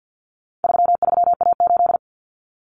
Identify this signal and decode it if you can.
Morse code, heard as a keyed tone, about 50 wpm, 700 Hz. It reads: 33A8
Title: Common Cut Numbers - 4-Digit 50wpm